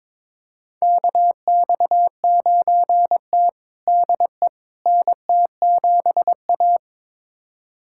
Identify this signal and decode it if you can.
Morse code, heard as a keyed tone, about 22 wpm, 700 Hz. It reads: KX9T DE NT7A
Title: USA Callsign DE Callsign - 4 Characters - 22wpm